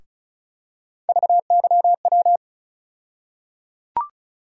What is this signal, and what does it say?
Morse code, heard as a keyed tone, about 36 wpm, 700 Hz. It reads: VYW
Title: USA Call Sign Postfixes - 3 Letters 35wpm